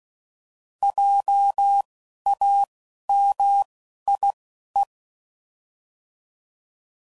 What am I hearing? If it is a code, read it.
Morse code, heard as a keyed tone, about 16 wpm, spaced as for 12 wpm, 800 Hz. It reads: JAMIE